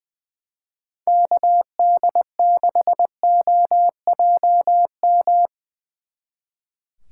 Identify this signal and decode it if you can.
Morse code, heard as a keyed tone, about 23 wpm, 700 Hz. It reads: KD6OJM
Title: Speed Racing - USA Call Signs 20wpm